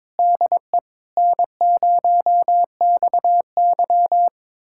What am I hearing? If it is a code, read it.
Morse code, heard as a keyed tone, about 22 wpm, 700 Hz. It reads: DE N0XY
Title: USA Callsign DE Callsign - 4 Characters - 22wpm